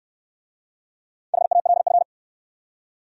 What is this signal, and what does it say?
Morse code, heard as a keyed tone, about 62 wpm, 700 Hz. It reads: HALF